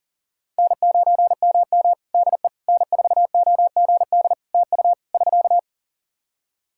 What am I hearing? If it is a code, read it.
Morse code, heard as a keyed tone, about 40 wpm, 700 Hz. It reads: N9MM DE N4OGD TU <SK>
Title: K1USN SST QSO Practice 40wpm